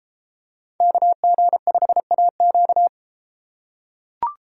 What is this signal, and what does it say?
Morse code, heard as a keyed tone, about 32 wpm, 700 Hz. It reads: KG5AQ